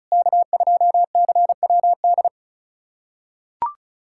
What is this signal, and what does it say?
Morse code, heard as a keyed tone, about 36 wpm, 700 Hz. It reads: K2CWD
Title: USA Callsign DE Callsign - 5 Characters - 35wpm